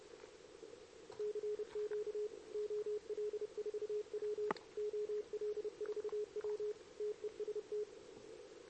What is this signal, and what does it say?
Morse code, heard as a keyed tone, about 31 wpm, 420 Hz. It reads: CQ OL4W OL4W TEST
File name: CQWWCW2023 / RUN